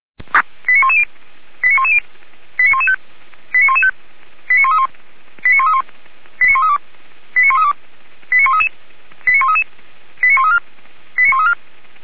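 Five-tone selective call (ZVEI), heard as quick runs of five tones, tones of about 70 ms. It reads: 89110 89110 89115 89115 89131 89131 89132 89132 89133 89133 89134 89134
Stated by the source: ZVEI five-tone call 89109 received by SDR.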